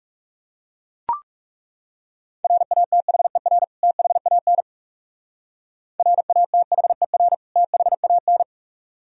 Morse code, heard as a keyed tone, about 37 wpm, 700 Hz. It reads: RATHER THAN RATHER THAN
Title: Speed-Racing - Sets of 2 Words 30wpm